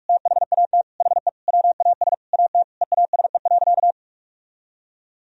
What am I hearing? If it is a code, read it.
Morse code, heard as a keyed tone, about 41 wpm, 700 Hz. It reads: THAT HE WAS AT EASE.
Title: Sentences from Top 1000 Words - Review Part 1 - 45wpm